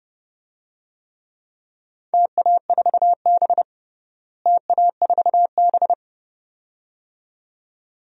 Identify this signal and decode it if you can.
Morse code, heard as a keyed tone, about 32 wpm, 700 Hz. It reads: TA4B TA4B
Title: DXCC-100 Call Signs - 4 Characters - Repeated 2x - 30wpm